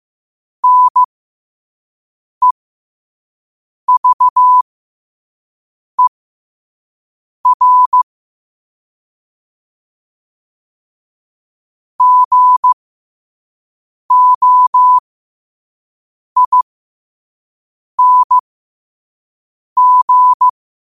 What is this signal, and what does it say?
Morse code, heard as a keyed tone, about 15 wpm, spaced as for 5 wpm, 1000 Hz. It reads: NEVER GOING
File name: qotdfiles / 5wpm